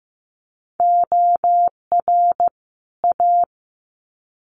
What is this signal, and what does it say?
Morse code, heard as a keyed tone, about 15 wpm, 700 Hz. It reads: OR A
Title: Sets of 2 Words 15wpm